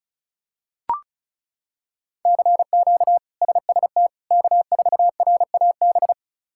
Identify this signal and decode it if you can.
Morse code, heard as a keyed tone, about 36 wpm, 700 Hz. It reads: CQ SST K4RAB
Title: K1USN SST QSO Practice - 35wpm